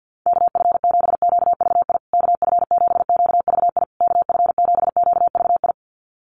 Morse code, heard as a keyed tone, about 50 wpm, 700 Hz. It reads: KF7QVI KF7QVI KF7QVI
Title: USA Call Signs Repeated 3x 50wpm